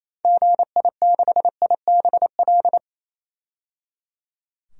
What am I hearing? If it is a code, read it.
Morse code, heard as a keyed tone, about 28 wpm, 700 Hz. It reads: GI6IBL
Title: DXCC-100 Call Signs - 6 Characters - 28wpm